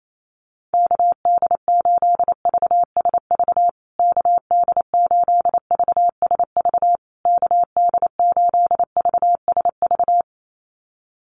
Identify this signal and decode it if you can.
Morse code, heard as a keyed tone, about 28 wpm, 700 Hz. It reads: KD8VSV KD8VSV KD8VSV
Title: USA Call Signs Repeated 3x 28wpm